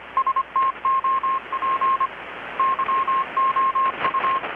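Morse code, heard as a keyed tone, about 25 wpm, 1070 Hz, with radio noise. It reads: STOP YOU
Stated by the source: Morse from KPH's Night of Nights XX maritime radiotelegraphy event, received in Chiapas, Mexico about 3,800 km away on a Kenwood R-600 with a grounded 10 m long-wire antenna.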